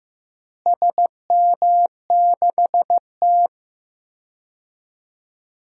Morse code, heard as a keyed tone, about 15 wpm, 700 Hz. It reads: SM6T
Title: DXCC-100 Call Signs - 4 Characters - Repeated 3x - 15wpm